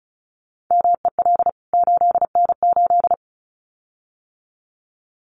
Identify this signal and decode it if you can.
Morse code, heard as a keyed tone, about 34 wpm, 700 Hz. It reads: MEL 8N8
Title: CWT Exchange 35wpm